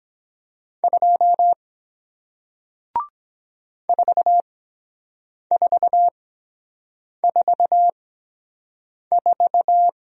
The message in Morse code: 2 4444

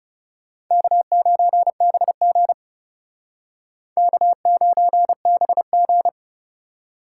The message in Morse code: K9BG K9BG

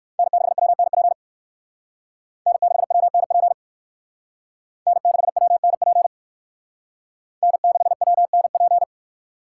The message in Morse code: N6WNP N6WNP N6WNP N6WNP

